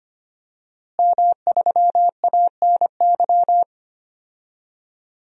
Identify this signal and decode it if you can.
Morse code, heard as a keyed tone, about 25 wpm, 700 Hz. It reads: M3ANY
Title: DXCC-100 Call Signs - 5 Characters - Repeated 3x - 25wpm